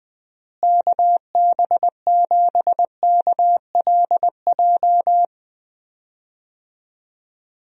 Morse code, heard as a keyed tone, about 19 wpm, 700 Hz. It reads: KB7KLJ
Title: USA Call Signs - 6 Characters - Repeated 2x - 20wpm